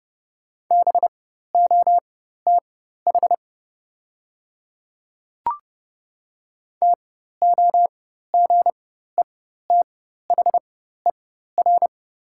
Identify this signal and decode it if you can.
Morse code, heard as a keyed tone, about 30 wpm, spaced as for 14 wpm, 700 Hz. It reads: BOTH TOGETHER
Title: 4x Character Spacing - Top 300 Words - 30wpm